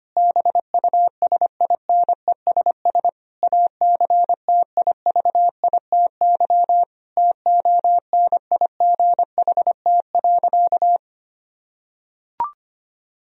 Morse code, heard as a keyed tone, about 25 wpm, 700 Hz. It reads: BUSINESS ACTIVITY TONIGHT.
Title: Sentences from Top 1000 Words - Review Part 4 - 25wpm